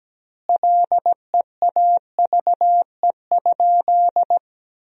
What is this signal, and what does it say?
Morse code, heard as a keyed tone, about 17 wpm, 700 Hz. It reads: LEAVE?
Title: 2x Word-Spacing - Sentences from Top 400 Words - 17wpm